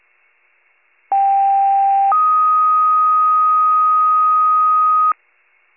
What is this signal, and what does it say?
Two-tone page, heard as a one-second tone, then a three-second tone, about 790 then 1250 Hz. It is heard from 1.1 to 5.1 s.